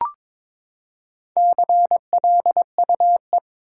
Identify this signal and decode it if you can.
Morse code, heard as a keyed tone, about 22 wpm, 700 Hz. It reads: CLUE
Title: Four Letter Words - 22wpm